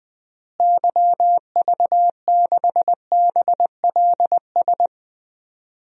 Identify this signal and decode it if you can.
Morse code, heard as a keyed tone, about 20 wpm, 700 Hz. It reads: YV6BLS
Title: DXCC-100 Call Signs - 6 Characters - Repeated 2x - 20wpm